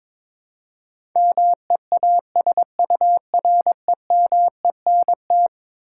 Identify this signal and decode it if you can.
Morse code, heard as a keyed tone, about 22 wpm, 700 Hz. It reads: MEASUREMENT
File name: practice / 22wpm / Eleven Letter Words - 22wpm